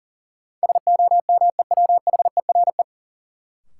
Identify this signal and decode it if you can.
Morse code, heard as a keyed tone, about 39 wpm, 700 Hz. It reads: SOMEWHERE
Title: Sentences from Top 1400 Words - New Word Drill Alt - 40wpm